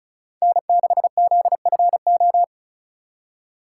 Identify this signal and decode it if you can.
Morse code, heard as a keyed tone, about 43 wpm, 700 Hz. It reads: N6ZFO
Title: Speed Racing - USA 1x3 Call Signs 35wpm